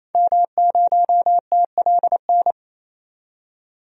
Morse code, heard as a keyed tone, about 27 wpm, 700 Hz. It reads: M0TLN
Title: DXCC-100 Call Signs - 5 Characters - 28wpm